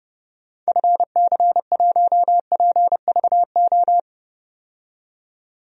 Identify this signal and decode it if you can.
Morse code, heard as a keyed tone, about 30 wpm, 700 Hz. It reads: FC1PVO
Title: DXCC-100 Call Signs - 6 Characters - 30wpm